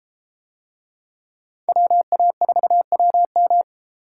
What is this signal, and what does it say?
Morse code, heard as a keyed tone, about 31 wpm, 700 Hz. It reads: WA4WM